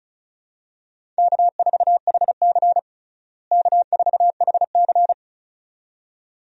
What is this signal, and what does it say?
Morse code, heard as a keyed tone, about 36 wpm, 700 Hz. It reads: K4HC K4HC
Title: USA Call Signs - 4 Characters - Repeated 2x - 35wpm